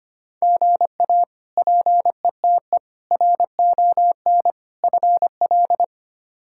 Morse code, heard as a keyed tone, about 25 wpm, 700 Hz. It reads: GA PETE RON FL
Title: K1USN SST QSO Practice - 25wpm